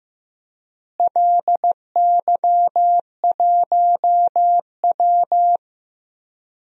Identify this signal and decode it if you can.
Morse code, heard as a keyed tone, about 16 wpm, 700 Hz. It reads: LY1W
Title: DXCC-100 Call Signs - 4 Characters - Repeated 3x - 15wpm